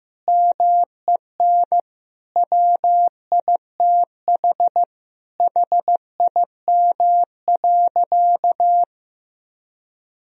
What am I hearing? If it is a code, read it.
Morse code, heard as a keyed tone, about 15 wpm, 700 Hz. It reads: MEN WITH HIM.
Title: Sentences from Top 1000 Words - Review Part 2 - 15wpm